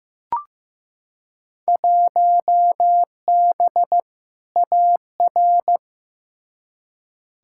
Morse code, heard as a keyed tone, about 15 wpm, 700 Hz. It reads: 1B AR
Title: Field Day Exchanges Encore - 15wpm